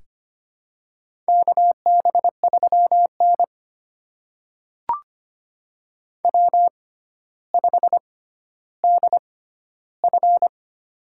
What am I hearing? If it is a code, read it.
Morse code, heard as a keyed tone, about 25 wpm, spaced as for 13 wpm, 700 Hz. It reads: KB3N W5DF
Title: USA Call Signs - 4 Characters - 6x Character Spacing - 25wpm